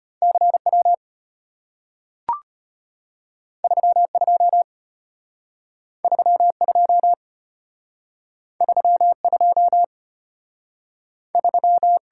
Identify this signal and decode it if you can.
Morse code, heard as a keyed tone, about 34 wpm, 700 Hz. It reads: CW 32 32 32 3